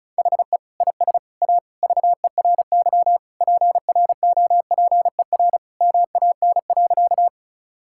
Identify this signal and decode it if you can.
Morse code, heard as a keyed tone, about 34 wpm, 700 Hz. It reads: HE IS A VERY PROPER MAN.
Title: Sentences from Top 1000 Words - Review Part 5 - 35wpm